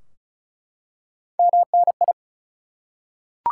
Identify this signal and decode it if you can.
Morse code, heard as a keyed tone, about 36 wpm, 700 Hz. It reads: MNI
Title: Top 180 QSO Elements 35wpm